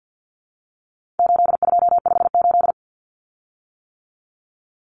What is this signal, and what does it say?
Morse code, heard as a keyed tone, about 50 wpm, 700 Hz. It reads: 8258
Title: Common Cut Numbers - 4-Digit 50wpm